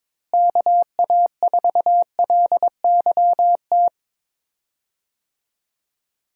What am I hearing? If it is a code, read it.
Morse code, heard as a keyed tone, about 21 wpm, 700 Hz. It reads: KA4LYT
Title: USA Call Signs - 6 Characters - Repeated 3x - 22wpm